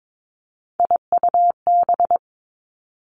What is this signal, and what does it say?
Morse code, heard as a keyed tone, about 22 wpm, 700 Hz. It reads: IUB
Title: Three Challenging Letter-Number Combinations 22wpm